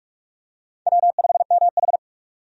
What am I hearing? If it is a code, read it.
Morse code, heard as a keyed tone, about 46 wpm, 700 Hz. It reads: W5MH
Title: USA Call Signs - 4 Characters - 45wpm